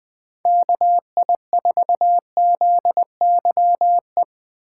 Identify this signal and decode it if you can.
Morse code, heard as a keyed tone, about 21 wpm, 700 Hz. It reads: KI4ZYE
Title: USA Call Signs - 3x Character Spacing - 20wpm